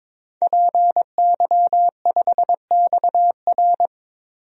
Morse code, heard as a keyed tone, about 22 wpm, 700 Hz. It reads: PY5XR
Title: DXCC-100 Call Signs - 5 Characters - 22wpm